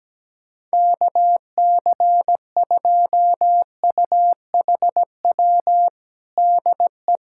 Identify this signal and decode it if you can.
Morse code, heard as a keyed tone, about 17 wpm, 700 Hz. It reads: KC2UHW DE
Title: Practice QSO - Exchange 4 - Ending - 17wpm